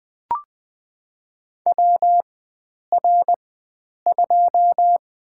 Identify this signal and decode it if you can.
Morse code, heard as a keyed tone, about 20 wpm, spaced as for 9 wpm, 700 Hz. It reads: WR2
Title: USA Call Signs - 4x Character Spacing - 20wpm